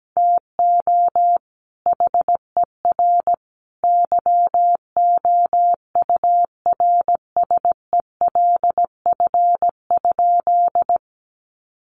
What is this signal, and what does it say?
Morse code, heard as a keyed tone, about 17 wpm, 700 Hz. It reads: TO HER YOURSELF?
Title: Sentences from Top 1500 Words 17wpm